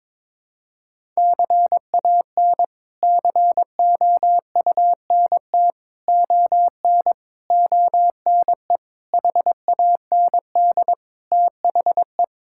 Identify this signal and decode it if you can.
Morse code, heard as a keyed tone, about 21 wpm, 700 Hz. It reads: CAN COUNT ON ONE HAND THE
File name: practice / 22wpm / Sets of 8 Words - 22wpm